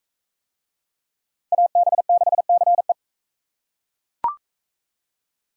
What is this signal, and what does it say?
Morse code, heard as a keyed tone, about 39 wpm, 700 Hz. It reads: AB6CE